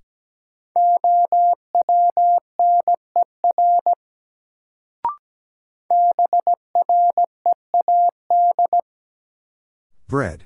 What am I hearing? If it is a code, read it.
Morse code, heard as a keyed tone, about 17 wpm, 700 Hz. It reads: OWNER BREAD
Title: Sentences from Top 1600 Words - New Word Drill - 17wpm